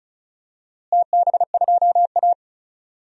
Morse code, heard as a keyed tone, about 34 wpm, 700 Hz. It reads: TB2A
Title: DXCC-100 Call Signs - 4 Characters - Repeated 3x - 35wpm